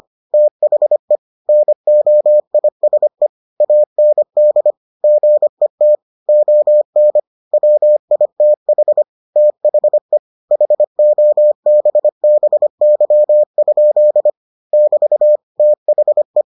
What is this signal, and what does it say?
Morse code, heard as a keyed tone, about 25 wpm, 600 Hz. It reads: THE NOISE AND GET ON WITH THE HOBBY? <BT> THE